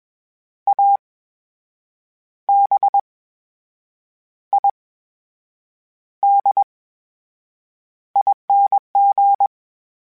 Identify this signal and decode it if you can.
Morse code, heard as a keyed tone, about 21 wpm, spaced as for 7 wpm, 800 Hz. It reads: ABIDING
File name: Special INGs-mp3s / Special INGs-mp3s / VerySlowToFaster Ing @20